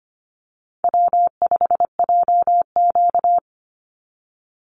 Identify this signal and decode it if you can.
Morse code, heard as a keyed tone, about 25 wpm, 700 Hz. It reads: W5JQ
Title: Call Signs Repeated 3x - 25wpm